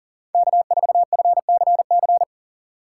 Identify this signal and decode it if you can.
Morse code, heard as a keyed tone, about 40 wpm, 700 Hz. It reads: K4FCC
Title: K1USN SST QSO Practice - 40wpm